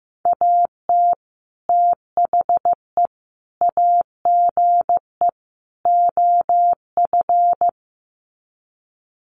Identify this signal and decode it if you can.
Morse code, heard as a keyed tone, about 15 wpm, 700 Hz. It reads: AT THE AGE OF
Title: Sets of 4 Words 15wpm